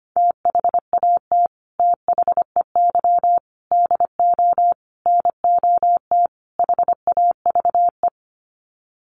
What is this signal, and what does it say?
Morse code, heard as a keyed tone, about 25 wpm, 700 Hz. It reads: THAT THEY DO NOT HAVE
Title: Sets of 5 Words Encore2 25wpm